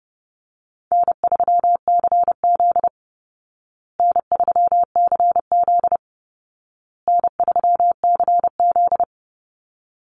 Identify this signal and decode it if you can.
Morse code, heard as a keyed tone, about 30 wpm, 700 Hz. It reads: N3CZ N3CZ N3CZ
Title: Call Signs Repeated 2x - 30wpm